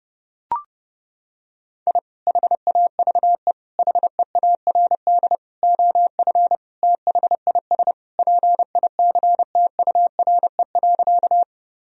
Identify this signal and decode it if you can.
Morse code, heard as a keyed tone, about 30 wpm, 700 Hz. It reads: I HAVE HEARD OF THIS PICTURE.
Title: Sentences from Top 1000 Words - Review Part 1 - 30wpm